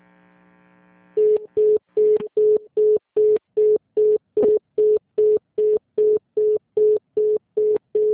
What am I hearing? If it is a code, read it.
Congestion tone (425 Hz).